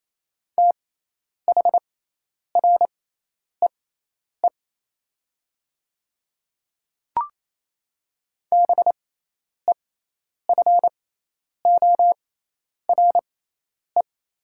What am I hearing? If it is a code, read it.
Morse code, heard as a keyed tone, about 28 wpm, spaced as for 10 wpm, 700 Hz. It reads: THREE BEFORE
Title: 6x Character Spacing - Top 200 Words - 28wpm